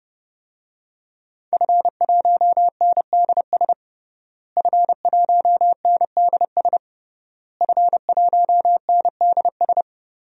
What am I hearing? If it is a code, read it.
Morse code, heard as a keyed tone, about 32 wpm, 700 Hz. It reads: F1NDS F1NDS F1NDS
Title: DXCC-100 Call Signs - 5 Characters - Repeated 3x - 30wpm